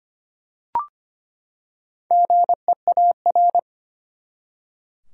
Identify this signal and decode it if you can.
Morse code, heard as a keyed tone, about 25 wpm, 700 Hz. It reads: GEAR
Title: Four Letter Words - 25wpm